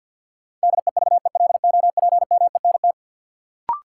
Morse code, heard as a keyed tone, about 46 wpm, 700 Hz. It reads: DEVELOPMENT
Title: Sets of 3 Words - New Word Drill - 50wpm